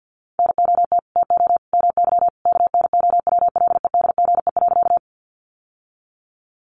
Morse code, heard as a keyed tone, about 50 wpm, 700 Hz. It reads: NOT TO MY KNOWLEDGE.
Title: Sentences from Top 600 Words 50wpm